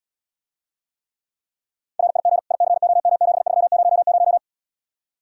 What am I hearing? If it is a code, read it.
Morse code, heard as a keyed tone, about 77 wpm, 700 Hz. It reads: RIG ICOM7300